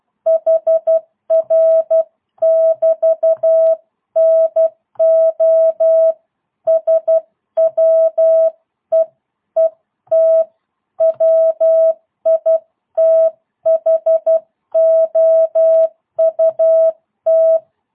Morse code, heard as a keyed tone, about 12 wpm, 650 Hz. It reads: HR <BT> NO SWEET WITHOUT
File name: jh8cbh / 2022cw